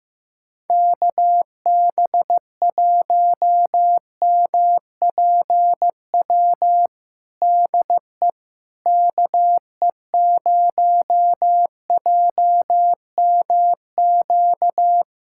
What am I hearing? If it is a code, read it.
Morse code, heard as a keyed tone, about 15 wpm, 700 Hz. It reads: KB1MPW DE KE0JMQ
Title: USA Callsign DE Callsign - 6 Characters - 15wpm